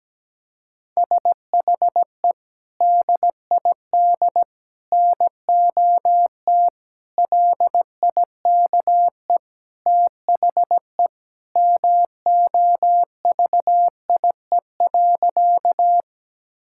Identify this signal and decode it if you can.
Morse code, heard as a keyed tone, about 17 wpm, 700 Hz. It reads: SHE DID NOT LIKE THE MOVIE.
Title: Sentences from Top 1000 Words - Review Part 5 - 17wpm